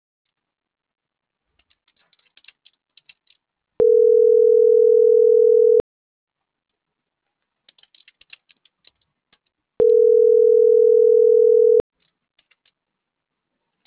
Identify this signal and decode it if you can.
Ringback tone (North America), 440 + 480 Hz.
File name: CENTRAL MAERSK / out gabrielle.melendez - .48111